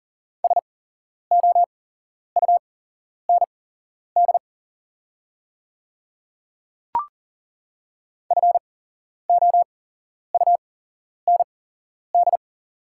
Morse code, heard as a keyed tone, about 40 wpm, spaced as for 11 wpm, 690 Hz. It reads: SOUND FOUND
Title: 8x Character Spacing - Top 500 Words - 40wpm